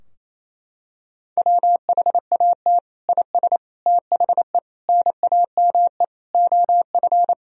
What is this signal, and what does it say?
Morse code, heard as a keyed tone, about 28 wpm, 700 Hz. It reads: WHAT IS THE NAME OF
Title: Sentences from Top 1000 Words - Review Part 2 - 28wpm